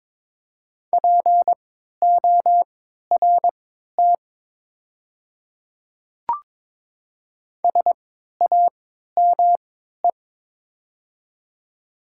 Morse code, heard as a keyed tone, about 22 wpm, spaced as for 12 wpm, 700 Hz. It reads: PORT SAME
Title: 3x Character Spacing - Top 200 Words - 22wpm